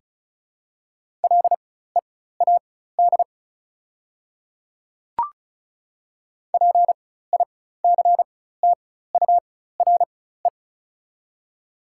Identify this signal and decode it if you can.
Morse code, heard as a keyed tone, about 34 wpm, spaced as for 16 wpm, 690 Hz. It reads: LEAD PICTURE
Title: 4x Character Spacing - Top 500 Words - 35wpm